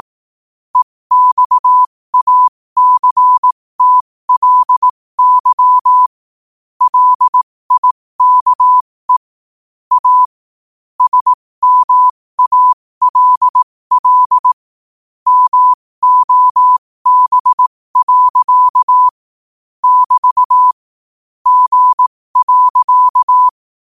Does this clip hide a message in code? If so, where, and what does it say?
Yes: Morse code from 0.8 to 23.5 s, heard as a keyed tone, about 18 wpm, spaced as for 15 wpm, 1000 Hz. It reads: EXACTLY LIKE A SMALL MOB. <BT> G.